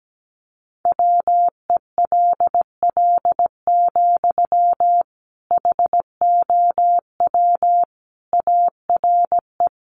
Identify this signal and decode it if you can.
Morse code, heard as a keyed tone, about 17 wpm, 700 Hz. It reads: WELL, HOW ARE
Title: Sentences from Top 1200 Words 17wpm